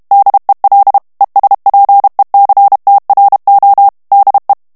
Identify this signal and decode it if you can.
Morse code, heard as a keyed tone, about 32 wpm, 800 Hz. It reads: DEL ESPECTRO DE
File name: ESPAGNOL TEXTE EN CLAIR 2 VITESSE 30 MOTS MINUTES